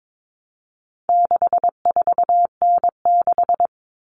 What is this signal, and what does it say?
Morse code, heard as a keyed tone, about 22 wpm, 700 Hz. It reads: 64N6
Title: All Cut Numbers - 4-Digit 22wpm